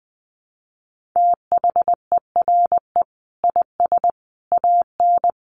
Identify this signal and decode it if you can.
Morse code, heard as a keyed tone, about 20 wpm, 700 Hz. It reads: THERE IS AN
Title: Sentences from Top 1400 Words 20wpm